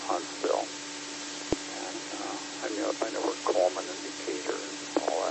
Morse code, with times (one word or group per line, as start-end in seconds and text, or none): none